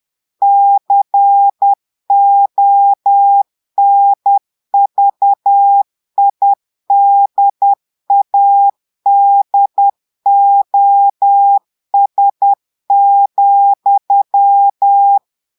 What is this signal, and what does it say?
Morse code, heard as a keyed tone, about 10 wpm, 800 Hz. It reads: CONVIDADOS,
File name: Portugais - texte en clair 1 - vitesse - 10 mots minute